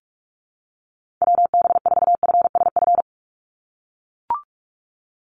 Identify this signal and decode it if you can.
Morse code, heard as a keyed tone, about 44 wpm, 700 Hz. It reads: WB4FSF
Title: Call Signs - 45wpm